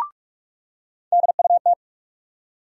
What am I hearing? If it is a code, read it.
Morse code, heard as a keyed tone, about 43 wpm, 700 Hz. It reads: DUT
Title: USA Call Sign Postfixes - 3 Letters 45wpm